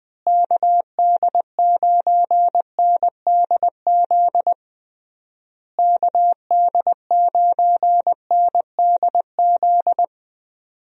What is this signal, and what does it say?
Morse code, heard as a keyed tone, about 20 wpm, 700 Hz. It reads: KD9NDZ KD9NDZ
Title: USA Call Signs - 6 Characters - Repeated 3x - 20wpm